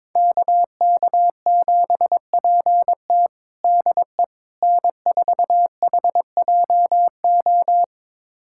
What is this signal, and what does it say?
Morse code, heard as a keyed tone, about 22 wpm, 700 Hz. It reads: KK7PT DE N4HJO
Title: USA Callsign DE Callsign - 5 Characters - 22wpm